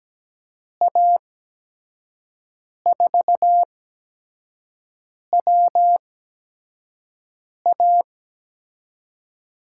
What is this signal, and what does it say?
Morse code, heard as a keyed tone, about 17 wpm, spaced as for 5 wpm, 700 Hz. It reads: A4WA
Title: USA Call Signs - 5 Characters - 8x Character Spacing - 17wpm